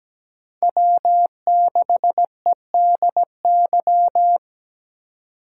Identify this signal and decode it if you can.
Morse code, heard as a keyed tone, about 17 wpm, 700 Hz. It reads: W6EDY
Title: K1USN SST QSO Practice - 17wpm